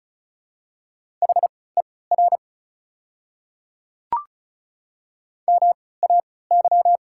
Morse code, heard as a keyed tone, about 34 wpm, spaced as for 20 wpm, 700 Hz. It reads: HER MAY
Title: 3x Character Spacing - Top 500 Words - 35wpm